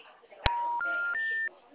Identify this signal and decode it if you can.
Special information tone, with speech.